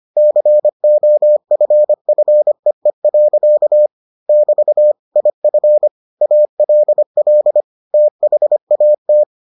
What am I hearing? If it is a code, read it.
Morse code, heard as a keyed tone, about 25 wpm, 600 Hz. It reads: COFFEE. <BT> IF ALL THAT